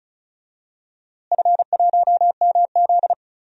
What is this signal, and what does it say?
Morse code, heard as a keyed tone, about 34 wpm, 700 Hz. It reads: F1MZ
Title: DXCC-100 Call Signs - 4 Characters - Repeated 3x - 35wpm